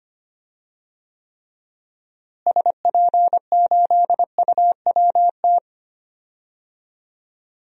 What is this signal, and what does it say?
Morse code, heard as a keyed tone, about 25 wpm, 700 Hz. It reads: SP8UWT
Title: DXCC-100 Call Signs - 6 Characters - Repeated 2x - 25wpm